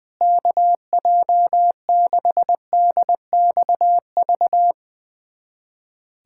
Morse code, heard as a keyed tone, about 20 wpm, 700 Hz. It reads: KJ6DXV
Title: USA Callsign DE Callsign 20wpm